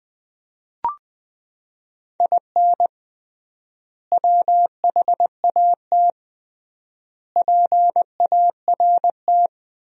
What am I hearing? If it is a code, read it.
Morse code, heard as a keyed tone, about 20 wpm, 700 Hz. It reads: IN WHAT PART
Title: 3x Word-Spacing - Sentences from Top 200 Words - 20wpm